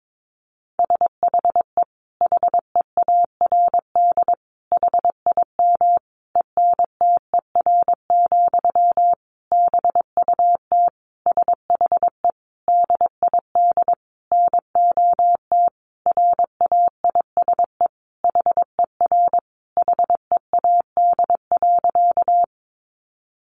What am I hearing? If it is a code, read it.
Morse code, heard as a keyed tone, about 22 wpm, 700 Hz. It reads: SHE HEARD HIM ENTER, BUT SHE DID NOT RAISE HER HEAD.
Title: Sentences from Top 1400 Words 22wpm